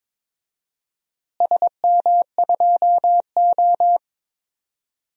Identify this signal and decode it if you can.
Morse code, heard as a keyed tone, about 22 wpm, 700 Hz. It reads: SM2O
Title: DXCC-100 Call Signs - 4 Characters - Repeated 3x - 22wpm